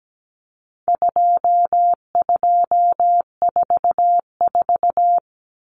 Morse code, heard as a keyed tone, about 17 wpm, 700 Hz. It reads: 2244
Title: Common Cut Numbers - 4-Digit 17wpm